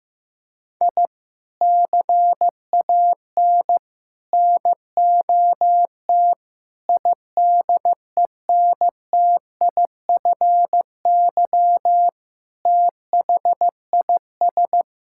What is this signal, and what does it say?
Morse code, heard as a keyed tone, about 15 wpm, 700 Hz. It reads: I CAN NOT IDENTIFY THIS
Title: Sentences from Top 1000 Words - Review Part 6 - 15wpm